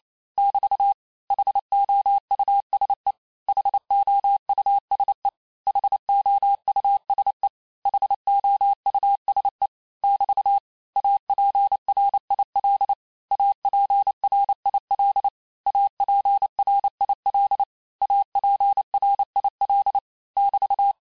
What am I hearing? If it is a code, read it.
Morse code, heard as a keyed tone, about 28 wpm, 800 Hz. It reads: <BT> HOUSE HOUSE HOUSE HOUSE <BT> APRIL APRIL APRIL APRIL <BT>
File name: cours-du-mercredi / audios / cours-865-teste-les-mots-vitesse-27mpm